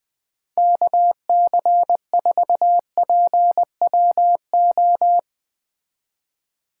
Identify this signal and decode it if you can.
Morse code, heard as a keyed tone, about 20 wpm, 700 Hz. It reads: KC4PWO